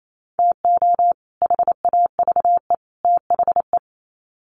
Morse code, heard as a keyed tone, about 28 wpm, 700 Hz. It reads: TO HAVE THE
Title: Sets of 3 Words 28wpm